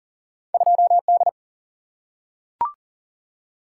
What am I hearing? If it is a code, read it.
Morse code, heard as a keyed tone, about 40 wpm, 700 Hz. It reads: 2D
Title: Single-Letter Single-Number 40wpm